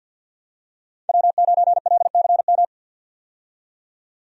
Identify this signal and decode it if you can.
Morse code, heard as a keyed tone, about 51 wpm, 700 Hz. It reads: W9LCM
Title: USA Call Signs - 5 Characters - 50wpm